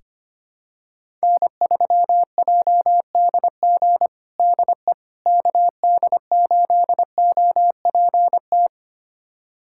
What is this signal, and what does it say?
Morse code, heard as a keyed tone, about 25 wpm, 700 Hz. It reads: N3JDG DE KD8OPT
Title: USA Callsign DE Callsign 25wpm